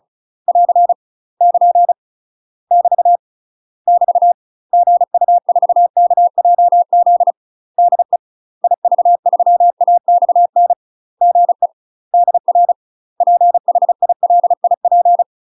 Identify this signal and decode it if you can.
Morse code, heard as a keyed tone, about 34 wpm, 700 Hz. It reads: <AR> <KN> <BT> <BT> GU4KJZ DE IV3AXN GE DR PHILIP
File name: UK MASTER QSOs TRAINING / 35WPM / UK MASTER QSOs 0021